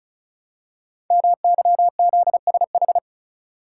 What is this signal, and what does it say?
Morse code, heard as a keyed tone, about 36 wpm, 700 Hz. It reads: MYZSH
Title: Code Groups - 5 Letter:Numbers - 35wpm